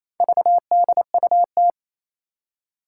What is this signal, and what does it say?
Morse code, heard as a keyed tone, about 28 wpm, 700 Hz. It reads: VDUT
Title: Four Challenging Letter-Number Combinations 28wpm